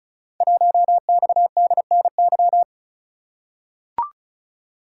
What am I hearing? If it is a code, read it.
Morse code, heard as a keyed tone, about 36 wpm, 700 Hz. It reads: 1XDNY
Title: Code Groups - 5 Letter:Numbers - 35wpm